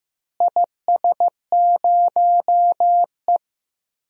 Morse code, heard as a keyed tone, about 15 wpm, 700 Hz. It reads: IS0E
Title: DXCC-100 Call Signs - 4 Characters - Repeated 2x - 15wpm